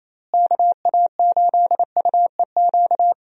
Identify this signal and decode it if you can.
Morse code, heard as a keyed tone, about 28 wpm, 700 Hz. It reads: KA8UEQ
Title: USA Call Signs - 6 Characters - Repeated 3x - 28wpm